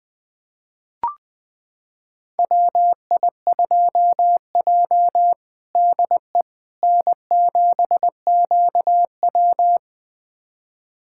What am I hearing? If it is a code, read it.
Morse code, heard as a keyed tone, about 20 wpm, 700 Hz. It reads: WI2J DE N7QW
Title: USA Callsign DE Callsign - 4 Characters - 20wpm